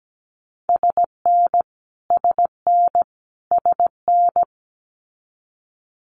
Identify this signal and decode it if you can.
Morse code, heard as a keyed tone, about 17 wpm, 700 Hz. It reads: SN SN SN
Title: Multi-letter Phonemes Repeated 3x 17wpm